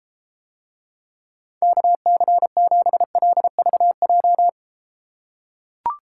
Morse code, heard as a keyed tone, about 33 wpm, 700 Hz. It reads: KC7LVJ